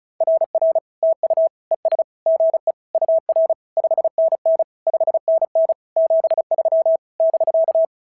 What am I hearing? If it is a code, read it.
Morse code, heard as a keyed tone, about 35 wpm, 650 Hz. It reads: RR TU ES GE UR 5NN 5NN 73 <BK>
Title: SOTA QSO Practice 35wpm